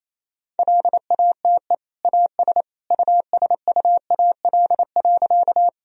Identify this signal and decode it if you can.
Morse code, heard as a keyed tone, about 28 wpm, 700 Hz. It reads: LATE AS USUAL.
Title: Sentences from Top 1000 Words - Review Part 2 - 28wpm